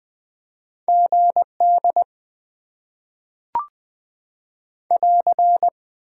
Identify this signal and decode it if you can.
Morse code, heard as a keyed tone, about 20 wpm, 700 Hz. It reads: GD <AR>
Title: Top 90 QSO Elements - 6x Character Spacing - 20wpm